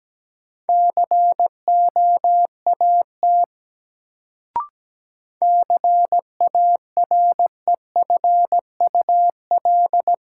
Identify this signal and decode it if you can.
Morse code, heard as a keyed tone, about 17 wpm, 700 Hz. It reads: COAT CAREFUL
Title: Sentences from Top 1600 Words - New Word Drill - 17wpm